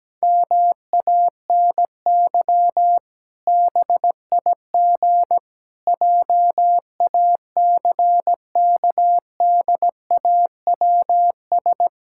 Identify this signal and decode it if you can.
Morse code, heard as a keyed tone, about 17 wpm, 700 Hz. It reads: MANY BIG JACKDAWS
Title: Pangram Challenge v2 - 17wpm